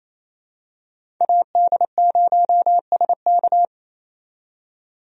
Morse code, heard as a keyed tone, about 28 wpm, 700 Hz. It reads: AD0SK
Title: USA Call Signs - 5 Characters - 28wpm